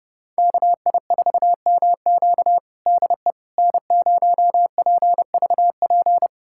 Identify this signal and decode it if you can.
Morse code, heard as a keyed tone, about 30 wpm, 700 Hz. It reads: KI4MQ DE N0PVP
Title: USA Callsign DE Callsign - 5 Characters - 30wpm